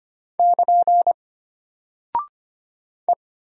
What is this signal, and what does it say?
Morse code, heard as a keyed tone, about 25 wpm, 700 Hz. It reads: <KN> E
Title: Single Character (Letter:Number:Punctuation:Prosign) Alt 25wpm